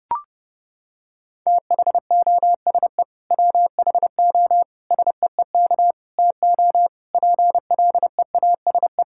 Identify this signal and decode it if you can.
Morse code, heard as a keyed tone, about 30 wpm, 700 Hz. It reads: THOSE WHO SEEK TO PLEASE
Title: Sentences from Top 1600 Words 30wpm